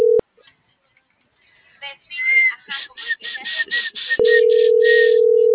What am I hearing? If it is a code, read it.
Ringback tone (North America), 440 + 480 Hz.